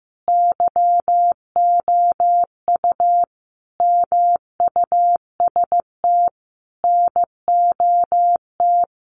Morse code, heard as a keyed tone, about 15 wpm, 700 Hz. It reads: YOU MUST NOT
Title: Sentences from Top 1200 Words 15wpm